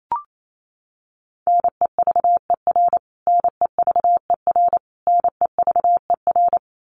Morse code, heard as a keyed tone, about 28 wpm, 700 Hz. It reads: NEVER NEVER NEVER
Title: Top 300 Words Repeated 28wpm